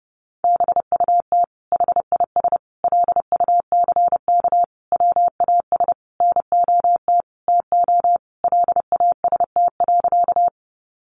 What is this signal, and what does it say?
Morse code, heard as a keyed tone, about 28 wpm, 700 Hz. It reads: BUT HIS LUCK WAS NOT TO LAST.
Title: Sentences from Top 1500 Words 30wpm